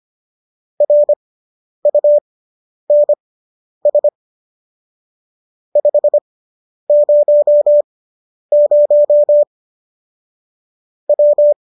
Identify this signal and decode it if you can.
Morse code, heard as a keyed tone, about 25 wpm, spaced as for 10 wpm, 590 Hz. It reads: RUNS 500 W